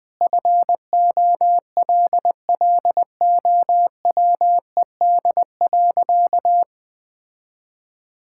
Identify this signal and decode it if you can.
Morse code, heard as a keyed tone, about 20 wpm, 700 Hz. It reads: FOLLOWED.
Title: Sentences with 5-Word Phrases - 20wpm